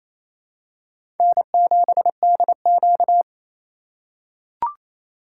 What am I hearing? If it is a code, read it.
Morse code, heard as a keyed tone, about 27 wpm, 700 Hz. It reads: N7DQ
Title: USA Call Signs - 4 Characters - 4x Character Spacing - 28wpm